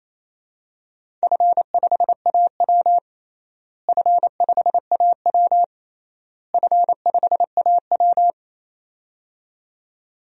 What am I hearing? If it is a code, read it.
Morse code, heard as a keyed tone, about 28 wpm, 700 Hz. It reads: F5AW F5AW F5AW
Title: DXCC-100 Call Signs - 4 Characters - Repeated 3x - 28wpm